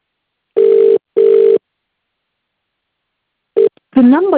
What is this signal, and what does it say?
Ringback tone (UK), 400 + 450 Hz, with speech.